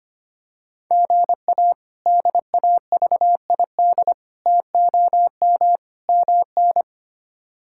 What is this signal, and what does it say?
Morse code, heard as a keyed tone, about 25 wpm, 700 Hz. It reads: GA DAVID TOM MN
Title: K1USN SST QSO Practice - 25wpm